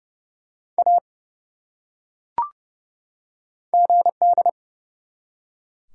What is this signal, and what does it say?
Morse code, heard as a keyed tone, about 30 wpm, 700 Hz. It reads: A GD
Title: Top 180 QSO Elements 30wpm